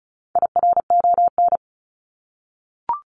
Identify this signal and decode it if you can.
Morse code, heard as a keyed tone, about 34 wpm, 700 Hz. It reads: IRON
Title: Top 1751-2000 Words 35wpm